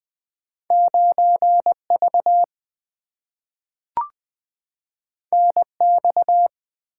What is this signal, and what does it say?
Morse code, heard as a keyed tone, about 20 wpm, 700 Hz. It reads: 9 V N X
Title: Two Letter-Number Combinations - 20wpm